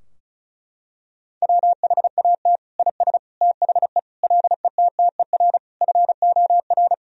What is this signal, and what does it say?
Morse code, heard as a keyed tone, about 36 wpm, 700 Hz. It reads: WHAT IS THE LETTER FOR
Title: Sentences from Top 1000 Words - Review Part 2 - 35wpm